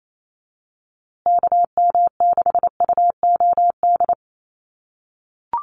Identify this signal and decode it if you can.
Morse code, heard as a keyed tone, about 28 wpm, 700 Hz. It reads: KM6UOD
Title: USA Call Signs Repeated 2x 28wpm